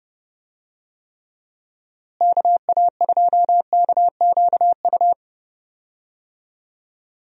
Morse code, heard as a keyed tone, about 32 wpm, 700 Hz. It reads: KA2KQU
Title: USA Call Signs - 6 Characters - Repeated 2x - 30wpm